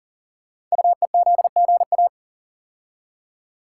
Morse code, heard as a keyed tone, about 39 wpm, 690 Hz. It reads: UEZGA
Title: Code Groups - 5 Letter:Numbers - 40wpm